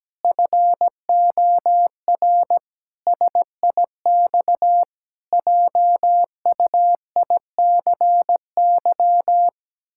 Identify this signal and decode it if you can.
Morse code, heard as a keyed tone, about 17 wpm, 700 Hz. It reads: FOR SIX JUICY
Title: Pangram Challenge v2 - 17wpm